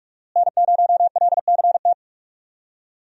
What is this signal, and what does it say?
Morse code, heard as a keyed tone, about 45 wpm, 700 Hz. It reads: N0LCT
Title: USA Call Signs - 5 Characters - Repeated 2x - 45wpm